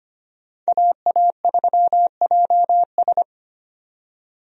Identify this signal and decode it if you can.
Morse code, heard as a keyed tone, about 25 wpm, 700 Hz. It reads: AA3JS
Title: USA Call Signs - 5 Characters - 25wpm